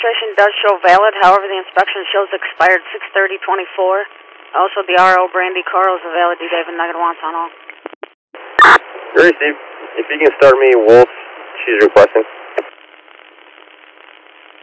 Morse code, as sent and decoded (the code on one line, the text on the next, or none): none